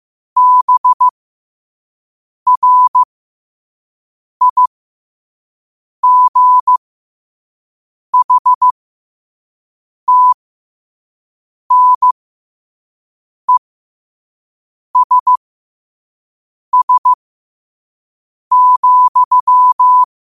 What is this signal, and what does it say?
Morse code, heard as a keyed tone, about 15 wpm, spaced as for 5 wpm, 1000 Hz. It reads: BRIGHTNESS,
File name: qotdfiles / 5wpm